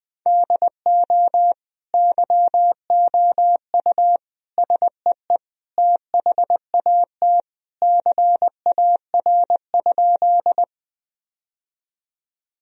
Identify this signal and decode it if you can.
Morse code, heard as a keyed tone, about 20 wpm, 700 Hz. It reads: DO YOU SEE THAT CAR?
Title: Sentences from Top 1000 Words - Review Part 1 - 20wpm